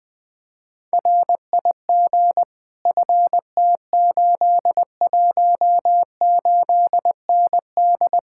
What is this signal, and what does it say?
Morse code, heard as a keyed tone, about 20 wpm, 700 Hz. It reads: RIG FT818ND
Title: Speed-Racing Common QSO Phrases - 17wpm